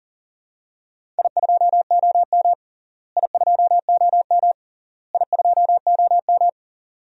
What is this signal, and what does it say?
Morse code, heard as a keyed tone, about 37 wpm, 700 Hz. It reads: I2OM I2OM I2OM
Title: DXCC-100 Call Signs - 4 Characters - Repeated 3x - 40wpm